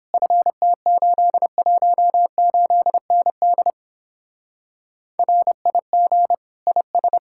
Morse code, heard as a keyed tone, about 29 wpm, 700 Hz. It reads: FT818ND RIG IS